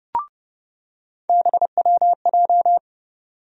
Morse code, heard as a keyed tone, about 30 wpm, 700 Hz. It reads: BWJ
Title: USA Call Sign Postfixes - 3 Letters 30wpm